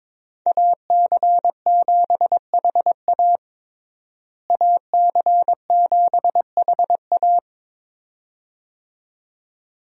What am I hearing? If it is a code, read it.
Morse code, heard as a keyed tone, about 22 wpm, 700 Hz. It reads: AC7HA AC7HA
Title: USA Call Signs - 5 Characters - Repeated 2x - 22wpm